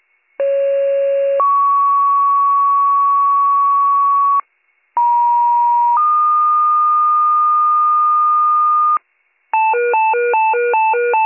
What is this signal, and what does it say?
Two-tone page, heard as a one-second tone, then a three-second tone, about 550 then 1120 Hz, about 950 then 1220 Hz. It is heard from 0.4 to 9.0 s.